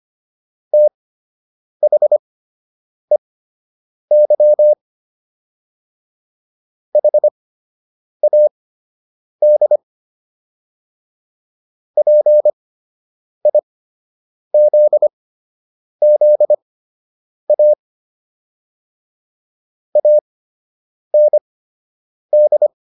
Morse code, heard as a keyed tone, about 25 wpm, spaced as for 8 wpm, 600 Hz. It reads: THEY HAD PIZZA AND